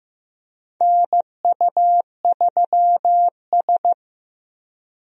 Morse code, heard as a keyed tone, about 16 wpm, 700 Hz. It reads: NU3S
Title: USA Call Signs - 4 Characters - Repeated 3x - 15wpm